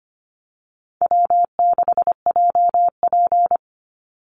Morse code, heard as a keyed tone, about 25 wpm, 700 Hz. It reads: W6JP
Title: NA SOTA Chaser Callsigns 25wpm